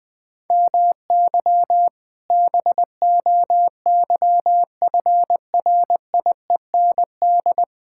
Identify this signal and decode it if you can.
Morse code, heard as a keyed tone, about 20 wpm, 700 Hz. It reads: MY BOYFRIEND
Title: Sentences from Top 1000 Words - Review Part 5 - 20wpm